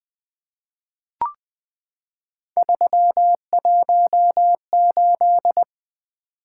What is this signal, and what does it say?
Morse code, heard as a keyed tone, about 20 wpm, 700 Hz. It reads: 318
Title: Three Letter-Number Combinations - 20wpm